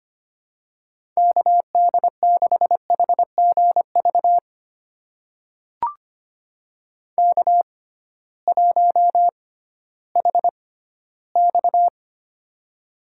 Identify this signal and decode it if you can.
Morse code, heard as a keyed tone, about 25 wpm, spaced as for 15 wpm, 700 Hz. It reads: KD6HGV K1HX
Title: USA Call Signs - 6x Character Spacing - 25wpm